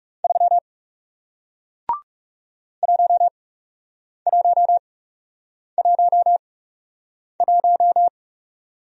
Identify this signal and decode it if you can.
Morse code, heard as a keyed tone, about 41 wpm, spaced as for 8 wpm, 700 Hz. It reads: 3 1111